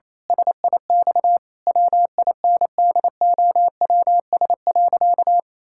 Morse code, heard as a keyed tone, about 27 wpm, 700 Hz. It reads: SIX WINDOWS.
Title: Pangram Challenge v2 - 28wpm